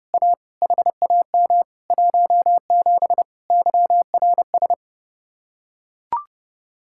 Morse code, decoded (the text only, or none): A HAM 17 YRS